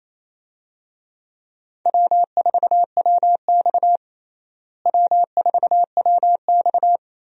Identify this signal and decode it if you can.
Morse code, heard as a keyed tone, about 28 wpm, 700 Hz. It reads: W4WX W4WX
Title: USA Call Signs - 4 Characters - Repeated 3x - 28wpm